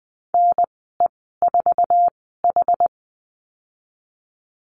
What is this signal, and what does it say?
Morse code, heard as a keyed tone, about 20 wpm, spaced as for 15 wpm, 700 Hz. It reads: NE4H
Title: 062 Lesson 11g - Letter h - Call Signs 20wpm